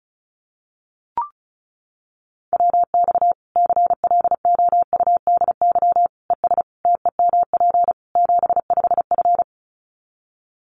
Morse code, heard as a keyed tone, about 34 wpm, 700 Hz. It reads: WX CLOUDY ES TEMP 75F
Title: Common QSO Phrases 35wpm